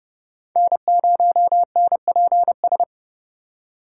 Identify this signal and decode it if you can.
Morse code, heard as a keyed tone, about 28 wpm, 700 Hz. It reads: N0NPS